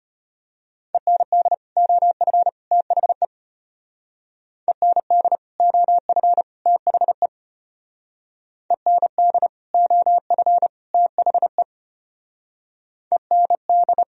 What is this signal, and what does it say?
Morse code, heard as a keyed tone, about 30 wpm, 700 Hz. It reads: END OF THE END OF THE END OF THE END
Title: Speed-Racing - Sets of 3 Words 25wpm